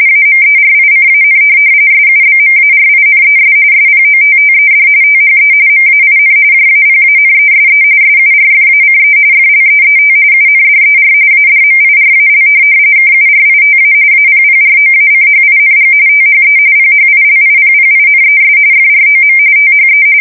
RTTY, about 45.45 baud. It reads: AMERICAN SUBMARINES SPOTTED THE MAIN JAPANESE SURFACE FLEET UNDER ADMIRAL KURITA APPROACHING THE PHILIPPINES FROM THE
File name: fsk170 / USS-New-Jersey